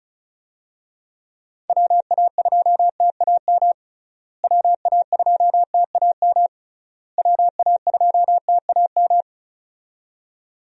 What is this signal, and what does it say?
Morse code, heard as a keyed tone, about 34 wpm, 700 Hz. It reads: WA2TAM WA2TAM WA2TAM
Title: USA Call Signs - 6 Characters - Repeated 3x - 35wpm